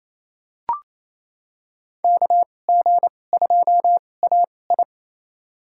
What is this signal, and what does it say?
Morse code, heard as a keyed tone, about 28 wpm, spaced as for 20 wpm, 700 Hz. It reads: KG2AI
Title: USA Call Signs - 5 Characters - 2x Character Spacing - 28wpm